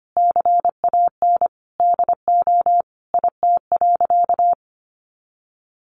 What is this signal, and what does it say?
Morse code, heard as a keyed tone, about 25 wpm, 700 Hz. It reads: CAN DO IT.
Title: Sentences from Top 100 Words 25wpm